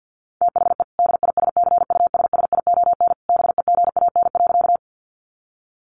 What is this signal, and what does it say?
Morse code, heard as a keyed tone, about 49 wpm, 700 Hz. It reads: THE DISCUSSION BEGAN.
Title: Sentences from Top 1100 Words 50wpm